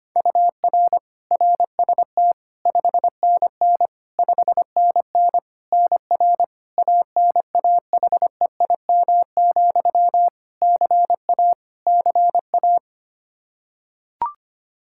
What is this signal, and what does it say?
Morse code, heard as a keyed tone, about 25 wpm, 700 Hz. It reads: UR RST 5NN 5NN NR ANAHEIM, CA CA
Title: Practice QSO - Exchange 1 - Beginning - 25wpm